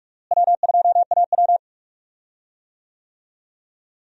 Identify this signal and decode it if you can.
Morse code, heard as a keyed tone, about 41 wpm, 700 Hz. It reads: W2AW
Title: USA Call Signs - 4 Characters - Repeated 3x - 45wpm